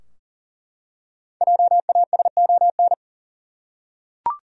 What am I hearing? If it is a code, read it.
Morse code, heard as a keyed tone, about 39 wpm, 700 Hz. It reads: JASON
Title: USA Common Names - 40wpm